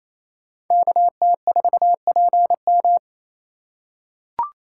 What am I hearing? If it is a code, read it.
Morse code, heard as a keyed tone, about 28 wpm, 700 Hz. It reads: KT4PM
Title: USA Call Signs - 3x Character Spacing - 28wpm